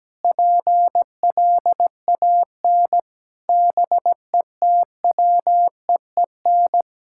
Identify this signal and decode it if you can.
Morse code, heard as a keyed tone, about 17 wpm, 700 Hz. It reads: PLAN BETWEEN
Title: Sentences from Top 1000 Words - Review Part 6 - 17wpm